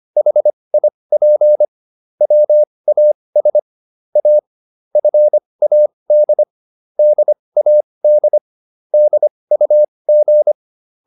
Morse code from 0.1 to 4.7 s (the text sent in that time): HIP WAS A